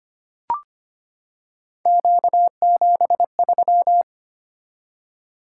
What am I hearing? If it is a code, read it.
Morse code, heard as a keyed tone, about 25 wpm, 700 Hz. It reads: Q73
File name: Three Letter-Number Combinations - 25wpm